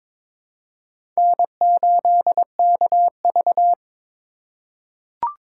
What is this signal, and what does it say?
Morse code, heard as a keyed tone, about 22 wpm, 700 Hz. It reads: N8KV
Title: USA Call Signs - 4 Characters - 2x Character Spacing - 22wpm